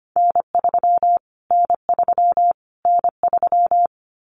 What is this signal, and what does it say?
Morse code, heard as a keyed tone, about 25 wpm, 700 Hz. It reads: N3 N3 N3
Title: Callsign Prefixes for Top 25 DXCC - Repeated 3x - 25wpm